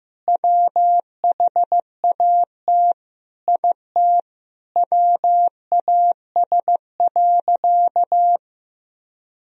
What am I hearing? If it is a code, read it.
Morse code, heard as a keyed tone, about 16 wpm, 700 Hz. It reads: WHAT IT WAS.
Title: Sentences from Top 1000 Words - Review Part 3 - 15wpm